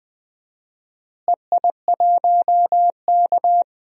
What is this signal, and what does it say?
Morse code, heard as a keyed tone, about 20 wpm, 700 Hz. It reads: EI1K
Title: DXCC-100 Call Signs - 4 Characters - Repeated 2x - 20wpm